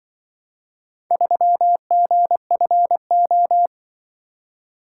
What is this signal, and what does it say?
Morse code, heard as a keyed tone, about 24 wpm, 700 Hz. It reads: 3GFO
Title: Speed-Racing - Four Letter-Number Combinations - 20wpm